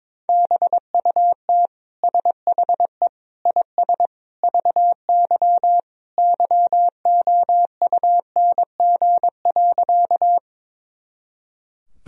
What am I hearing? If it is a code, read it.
Morse code, heard as a keyed tone, about 22 wpm, 700 Hz. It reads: BUT SHE IS VY YOUNG.
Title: Sentences from Top 400 Words and Abbreviations 22wpm